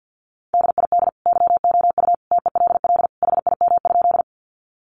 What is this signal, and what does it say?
Morse code, heard as a keyed tone, about 50 wpm, 700 Hz. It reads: DID YOU TELL HIM?
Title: Sentences from Top 200 Words 50wpm